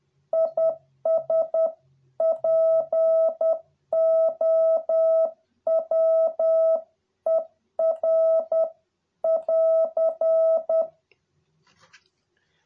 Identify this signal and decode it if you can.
Morse code, heard as a keyed tone, about 10 wpm, 650 Hz. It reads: IS POWER <AR>